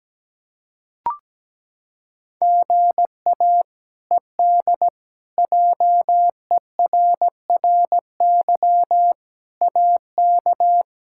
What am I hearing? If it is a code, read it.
Morse code, heard as a keyed tone, about 17 wpm, 700 Hz. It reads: GA ED JERRY AK
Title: K1USN SST QSO Practice - 17wpm